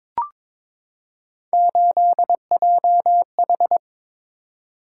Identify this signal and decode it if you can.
Morse code, heard as a keyed tone, about 23 wpm, 700 Hz. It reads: 8JH
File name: practice / 22wpm / Three Letter-Number Combinations - 22wpm